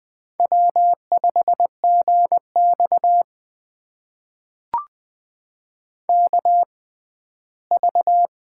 Morse code, heard as a keyed tone, about 20 wpm, spaced as for 14 wpm, 700 Hz. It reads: W5GX KV
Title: USA Call Signs - 4 Characters - 6x Character Spacing - 20wpm